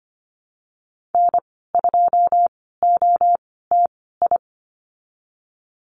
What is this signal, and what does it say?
Morse code, heard as a keyed tone, about 25 wpm, spaced as for 16 wpm, 700 Hz. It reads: N2OTI
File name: Lesson 15g - Number 5 - Call Signs 25wpm